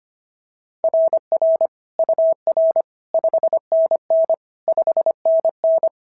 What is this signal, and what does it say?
Morse code, heard as a keyed tone, about 24 wpm, 650 Hz. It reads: RR UR 5NN 5NN
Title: SOTA QSO Practice 25wpm